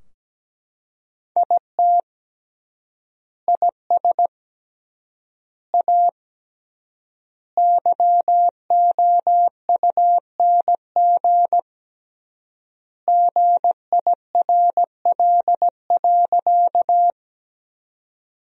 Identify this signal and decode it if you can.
Morse code, heard as a keyed tone, about 17 wpm, 700 Hz. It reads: IT IS A YOUNG GIRL.
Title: 3x Word-Spacing - Sentences from Top 400 Words - 17wpm